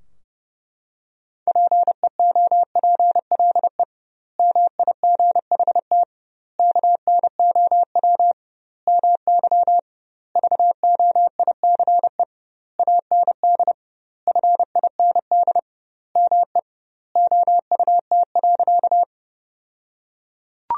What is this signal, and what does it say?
Morse code, heard as a keyed tone, about 30 wpm, 700 Hz. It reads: PEOPLE MIGHT KNOW MY VOICE AND FIND ME OUT.
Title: 2x Word-Spacing - Sentences from Top 500 Words - 30wpm